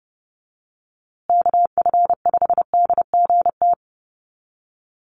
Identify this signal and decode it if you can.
Morse code, heard as a keyed tone, about 30 wpm, 700 Hz. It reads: KF5DGT
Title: USA Call Signs Repeated 3x 30wpm